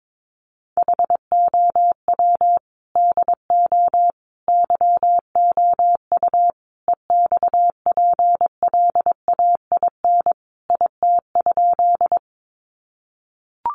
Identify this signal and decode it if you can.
Morse code, heard as a keyed tone, about 22 wpm, 700 Hz. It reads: HOW DO YOU EXPLAIN IT?
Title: Sentences from Top 1100 Words 22wpm